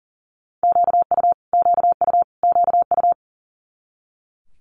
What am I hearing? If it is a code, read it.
Morse code, heard as a keyed tone, about 39 wpm, 700 Hz. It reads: QU QU QU
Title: Multi-letter Phonemes Repeated 3x 40wpm